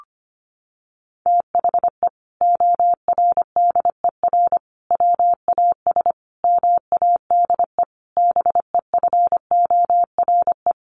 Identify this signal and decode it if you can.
Morse code, heard as a keyed tone, about 25 wpm, 700 Hz. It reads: THE ORDER WAS MADE BEFORE
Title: Sentences from Top 1500 Words 25wpm